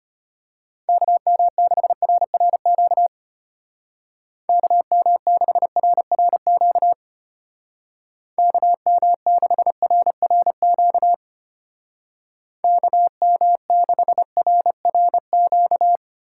Morse code, heard as a keyed tone, about 32 wpm, 700 Hz. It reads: KM6RRQ KM6RRQ KM6RRQ KM6RRQ